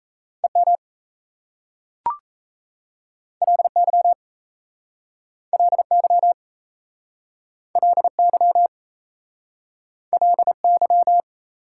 Morse code, heard as a keyed tone, about 38 wpm, 700 Hz. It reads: EM LY LY LY LY